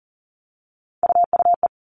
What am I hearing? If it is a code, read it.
Morse code, heard as a keyed tone, about 40 wpm, 700 Hz. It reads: UUE
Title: All Cut Numbers - 3-Digit 40wpm